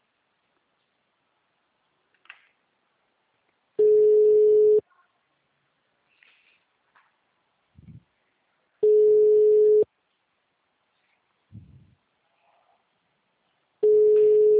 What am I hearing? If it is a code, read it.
Ringback tone (Europe, 425 Hz).